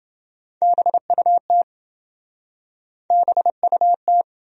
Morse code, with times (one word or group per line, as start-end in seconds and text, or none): 0.6-1.6 BUT
3.1-4.2 BUT